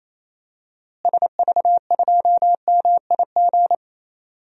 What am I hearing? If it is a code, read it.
Morse code, heard as a keyed tone, about 27 wpm, 700 Hz. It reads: SV2MIG
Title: DXCC-100 Call Signs - 6 Characters - Repeated 3x - 28wpm